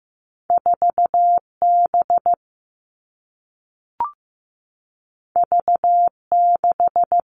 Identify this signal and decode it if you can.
Morse code, heard as a keyed tone, about 15 wpm, 700 Hz. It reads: 4B V6
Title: Two Challenging Letter-Number Combinations 15wpm